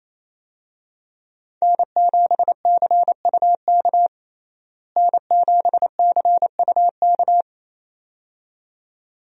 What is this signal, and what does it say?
Morse code, heard as a keyed tone, about 28 wpm, 700 Hz. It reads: N7CUK N7CUK
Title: USA Call Signs - 5 Characters - Repeated 2x - 28wpm